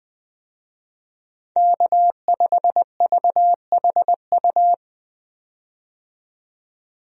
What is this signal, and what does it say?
Morse code, heard as a keyed tone, about 20 wpm, 700 Hz. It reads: K5VHU
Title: USA Call Signs - 5 Characters - 20wpm